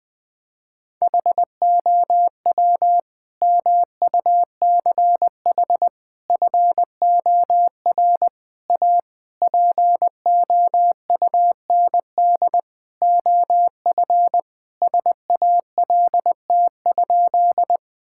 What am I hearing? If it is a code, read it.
Morse code, heard as a keyed tone, about 19 wpm, 700 Hz. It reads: HOW MUCH FOR A POUND OF SALT?
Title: Sentences from Top 1000 Words - Review Part 5 - 20wpm